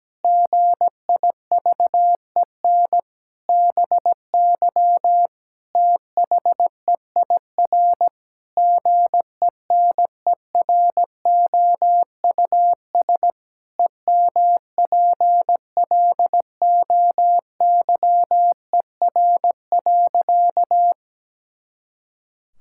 Morse code, heard as a keyed tone, about 17 wpm, 700 Hz. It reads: GIVEN BY THEIR GENEROUS EMPLOYER.
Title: Pangram Challenge v2 - 17wpm